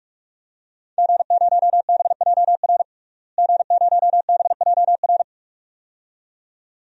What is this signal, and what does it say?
Morse code, heard as a keyed tone, about 41 wpm, 700 Hz. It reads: G0BJR G0BJR
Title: DXCC-100 Call Signs - 5 Characters - Repeated 2x - 45wpm